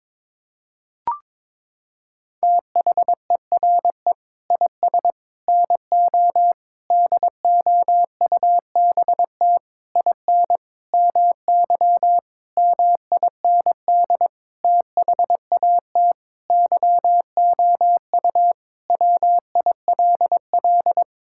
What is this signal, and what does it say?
Morse code, heard as a keyed tone, about 21 wpm, 700 Hz. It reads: THERE IS NO DOUBT IN MY MIND THAT YOU WILL
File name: practice / 22wpm / Sentences with 8-Word Phrases - 22wpm